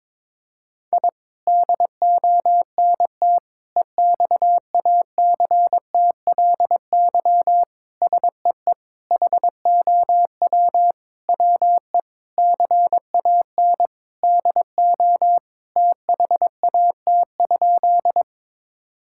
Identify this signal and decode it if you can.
Morse code, heard as a keyed tone, about 22 wpm, 700 Hz. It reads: I DONT EXACTLY SEE HOW WE CAN DO THAT?
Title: Sentences from Top 1000 Words - Review Part 6 - 22wpm